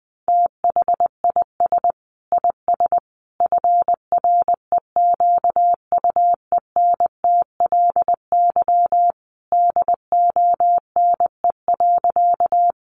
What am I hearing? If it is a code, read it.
Morse code, heard as a keyed tone, about 20 wpm, 700 Hz. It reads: THIS IS FREQUENTLY DONE.
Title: Sentences from Top 1300 Words 20wpm